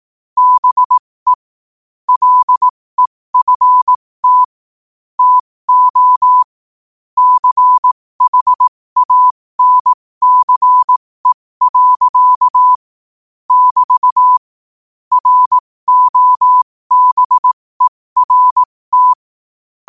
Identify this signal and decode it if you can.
Morse code, heard as a keyed tone, about 18 wpm, spaced as for 15 wpm, 1000 Hz. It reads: BE LEFT TO CHANCE. <BT> ROBERT
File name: qotdfiles / 15wpm